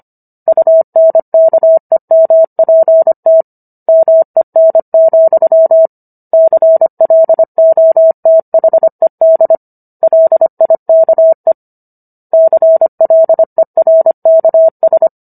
Morse code, heard as a keyed tone, about 25 wpm, 650 Hz. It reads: UNKEMPT MEN, CLOTHED LIKE CLERKS